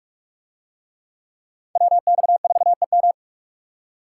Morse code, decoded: WX4EM

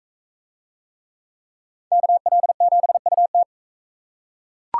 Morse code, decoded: KL7UT

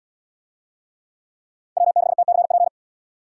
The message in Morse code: W6TCP

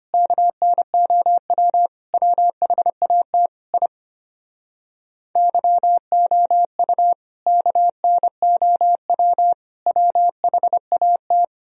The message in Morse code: KNOW WHAT I YOU KNOW WHAT

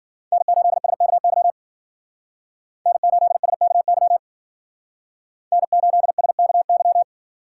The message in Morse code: N8SKY N8SKY N8SKY

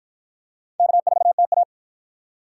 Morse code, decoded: K4TA

